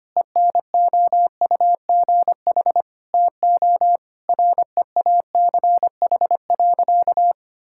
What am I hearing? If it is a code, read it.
Morse code, heard as a keyed tone, about 25 wpm, 700 Hz. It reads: ENOUGH TO REACH.
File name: practice / 25wpm / Sentences with 2-Word Phrases - 25wpm